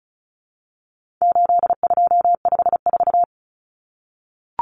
Morse code, heard as a keyed tone, about 34 wpm, 700 Hz. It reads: 8254
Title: Common Cut Numbers - 4-Digit 35wpm